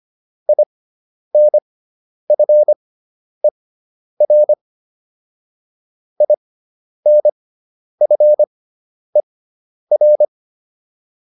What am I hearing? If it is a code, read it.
Morse code, heard as a keyed tone, about 25 wpm, spaced as for 10 wpm, 600 Hz. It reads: INFER INFER